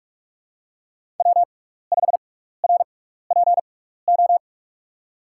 Morse code, heard as a keyed tone, about 44 wpm, spaced as for 15 wpm, 700 Hz. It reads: W5RPO
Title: USA Call Signs - 5 Characters - 6x Character Spacing - 45wpm